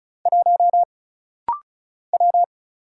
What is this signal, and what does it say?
Morse code, heard as a keyed tone, about 36 wpm, 700 Hz. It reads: 1 W
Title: Single Letter-Number Alt - Rapid-Fire 35wpm